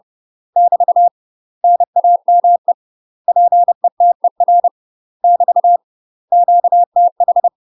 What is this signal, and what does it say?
Morse code, heard as a keyed tone, about 30 wpm, 700 Hz. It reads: <BT> NAME PETER <BT> QTH
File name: UK MASTER QSOs TRAINING / 30WPM / UK MASTER QSOs 0049